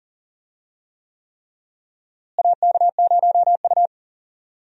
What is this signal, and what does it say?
Morse code, heard as a keyed tone, about 37 wpm, 700 Hz. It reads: AK0U
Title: USA Call Signs - 4 Characters - Repeated 3x - 40wpm